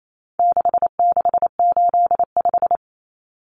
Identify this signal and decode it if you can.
Morse code, heard as a keyed tone, about 28 wpm, 700 Hz. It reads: 6685
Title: Common Cut Numbers - 4-Digit 28wpm